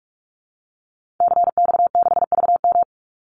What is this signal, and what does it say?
Morse code, heard as a keyed tone, about 44 wpm, 700 Hz. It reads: CX6VM
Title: Call Signs Repeated 2x - 45wpm